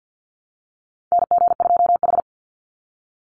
Morse code, heard as a keyed tone, about 49 wpm, 700 Hz. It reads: NG2H
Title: Call Signs Repeated 3x - 50wpm